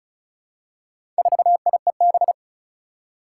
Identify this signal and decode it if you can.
Morse code, heard as a keyed tone, about 34 wpm, 700 Hz. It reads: 4IEB
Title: Four Challenging Letter-Number Combinations 35wpm